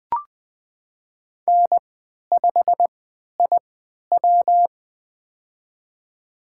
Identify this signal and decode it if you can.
Morse code, heard as a keyed tone, about 20 wpm, spaced as for 11 wpm, 700 Hz. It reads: N5IW
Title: USA Call Signs - 3x Character Spacing - 20wpm